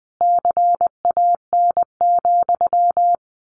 Morse code, heard as a keyed tone, about 20 wpm, 700 Hz. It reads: CAN,
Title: A Sound Of Thunder 20wpm